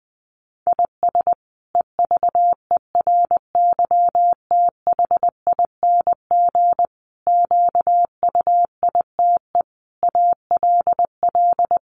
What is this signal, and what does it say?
Morse code, heard as a keyed tone, about 20 wpm, 700 Hz. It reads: IS EVERYTHING QUITE ALL
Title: Sentences from Top 700 Words 20wpm